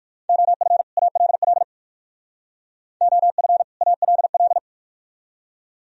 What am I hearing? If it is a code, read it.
Morse code, heard as a keyed tone, about 46 wpm, 700 Hz. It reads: OF ALL OF ALL
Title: Speed-Racing - Sets of 2 Words 45wpm